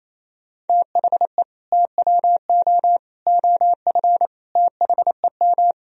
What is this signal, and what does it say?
Morse code, heard as a keyed tone, about 36 wpm, 700 Hz. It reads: THE TWO OF THEM